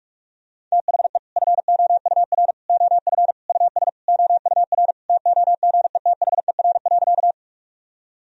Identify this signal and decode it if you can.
Morse code, heard as a keyed tone, about 46 wpm, 700 Hz. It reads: THE FOUR OF US OUR TOGETHER.
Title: Sentences from Top 1000 Words - Review Part 1 - 45wpm